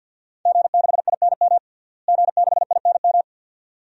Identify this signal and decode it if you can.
Morse code, heard as a keyed tone, about 51 wpm, 700 Hz. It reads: G6INM G6INM
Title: DXCC-100 Call Signs - 5 Characters - Repeated 2x - 50wpm